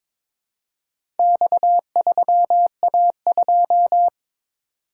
Morse code, heard as a keyed tone, about 22 wpm, 700 Hz. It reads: X3A2
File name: practice / 22wpm / Four Letter-Number Combinations - 22wpm